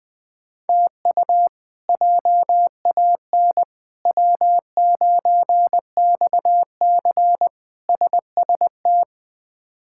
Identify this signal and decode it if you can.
Morse code, heard as a keyed tone, about 20 wpm, 700 Hz. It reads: TU JAN W9XC SST
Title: K1USN SST QSO Practice - 20wpm